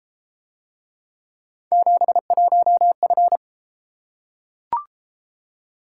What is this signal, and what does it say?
Morse code, heard as a keyed tone, about 30 wpm, 700 Hz. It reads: 71F